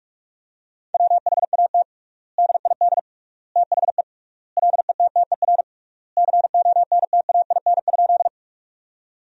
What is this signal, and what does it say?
Morse code, heard as a keyed tone, about 46 wpm, 700 Hz. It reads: WHAT DID THE LETTER CONTAIN?
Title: 3x Word-Spacing - Sentences from Top 500 Words - 45wpm